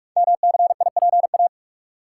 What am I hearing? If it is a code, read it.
Morse code, heard as a keyed tone, about 44 wpm, 690 Hz. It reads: MCIPA
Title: Code Groups - 5 Letters - 45wpm